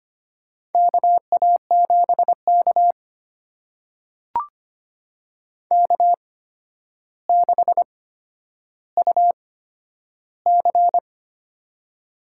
Morse code, heard as a keyed tone, about 25 wpm, spaced as for 11 wpm, 700 Hz. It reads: KA7K K6UC
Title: USA Call Signs - 4 Characters - 8x Character Spacing - 25wpm